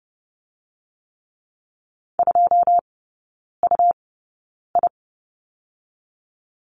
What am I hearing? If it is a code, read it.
Morse code, heard as a keyed tone, about 30 wpm, spaced as for 9 wpm, 690 Hz. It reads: 2UI
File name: Lesson 17e - Letter u - Three Characters 30wpm